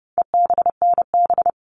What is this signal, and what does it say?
Morse code, heard as a keyed tone, about 30 wpm, 700 Hz. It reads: EBNB
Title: All Cut Numbers - 4-Digit 30wpm